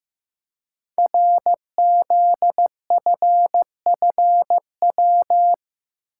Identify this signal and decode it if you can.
Morse code, heard as a keyed tone, about 15 wpm, 700 Hz. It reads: RZFFW
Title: Code Groups - 5 Letters - 15wpm